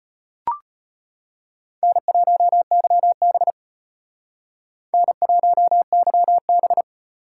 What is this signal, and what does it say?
Morse code, heard as a keyed tone, about 34 wpm, 700 Hz. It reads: N1YB N1YB